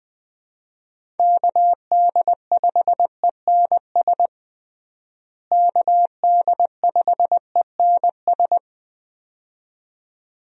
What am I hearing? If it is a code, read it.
Morse code, heard as a keyed tone, about 20 wpm, 700 Hz. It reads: KD5ENS KD5ENS
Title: USA Call Signs - 6 Characters - Repeated 2x - 20wpm